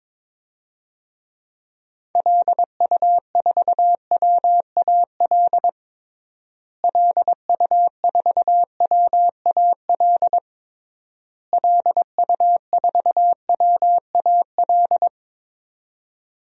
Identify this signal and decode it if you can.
Morse code, heard as a keyed tone, about 22 wpm, 700 Hz. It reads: LU4WAL LU4WAL LU4WAL
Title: DXCC-100 Call Signs - 6 Characters - Repeated 3x - 22wpm